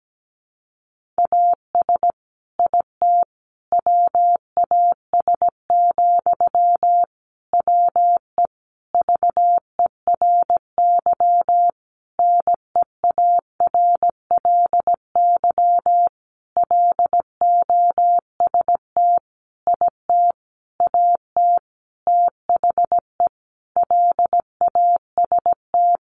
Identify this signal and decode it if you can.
Morse code, heard as a keyed tone, about 17 wpm, 700 Hz. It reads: AS IT WAS, WE VERY NEARLY LOST IT AT THE LAST
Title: Sentences from Top 1400 Words 17wpm